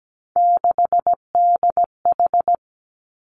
Morse code, heard as a keyed tone, about 17 wpm, 700 Hz. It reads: 6DH
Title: Three Challenging Letter-Number Combinations 17wpm